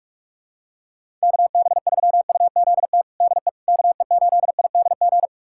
Morse code, heard as a keyed tone, about 47 wpm, 700 Hz. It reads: KB3UZT DE KE8IDG
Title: USA Callsign DE Callsign 45wpm